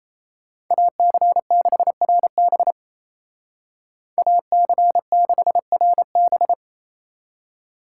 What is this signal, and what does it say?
Morse code, heard as a keyed tone, about 34 wpm, 700 Hz. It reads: AC6RB AC6RB
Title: Speed Racing - USA 2x2 Call Signs 28wpm